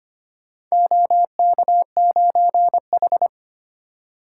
Morse code, heard as a keyed tone, about 25 wpm, 700 Hz. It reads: OK9H
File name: practice / 25wpm / Four Letter-Number Combinations - 25wpm